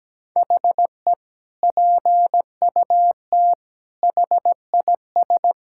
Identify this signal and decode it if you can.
Morse code, heard as a keyed tone, about 17 wpm, 700 Hz. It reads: HE PUT HIS
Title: Sentences from Top 1000 Words - Review Part 6 - 17wpm